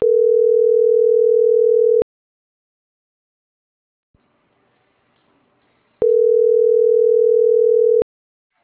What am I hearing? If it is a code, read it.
Ringback tone (North America), 440 + 480 Hz.